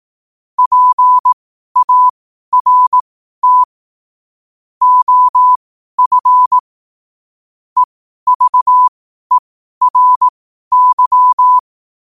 Morse code, heard as a keyed tone, about 18 wpm, spaced as for 12 wpm, 1000 Hz. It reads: PART OF EVERY
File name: qotdfiles / 12wpm